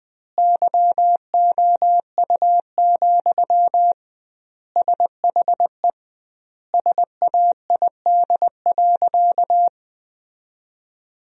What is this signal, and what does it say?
Morse code, heard as a keyed tone, about 20 wpm, 700 Hz. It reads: YOU, SHE SAID.
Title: 2x Word-Spacing - Sentences from Top 400 Words - 20wpm